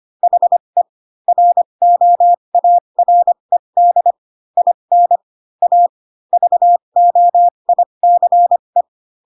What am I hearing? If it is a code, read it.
Morse code, heard as a keyed tone, about 25 wpm, spaced as for 22 wpm, 700 Hz. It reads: HE ROARED IN A VOICE
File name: Pinocchio / Pinocchio ch3 18 25 ik1hge